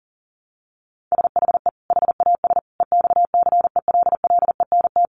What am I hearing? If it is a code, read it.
Morse code, heard as a keyed tone, about 40 wpm, 700 Hz. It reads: SHE HAS EXCELLENT
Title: Sentences from Top 1100 Words 40wpm